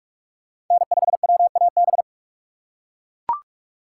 Morse code, heard as a keyed tone, about 42 wpm, 700 Hz. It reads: N5WAB
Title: USA Call Signs - 3x Character Spacing - 45wpm